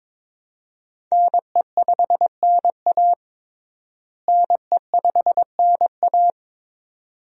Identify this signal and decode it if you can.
Morse code, heard as a keyed tone, about 22 wpm, 700 Hz. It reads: NE5NA NE5NA
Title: USA Call Signs - 5 Characters - Repeated 2x - 22wpm